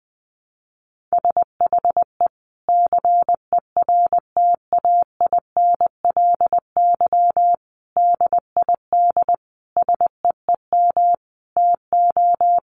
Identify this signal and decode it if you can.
Morse code, heard as a keyed tone, about 20 wpm, 700 Hz. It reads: SHE CERTAINLY DID SEEM TO
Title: Sentences from Top 1100 Words 20wpm